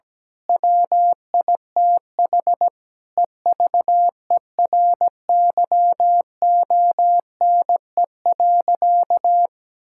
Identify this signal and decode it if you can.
Morse code, heard as a keyed tone, about 17 wpm, 700 Hz. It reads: WITH EVERYONE.
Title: Sentences from Top 1000 Words - Review Part 5 - 17wpm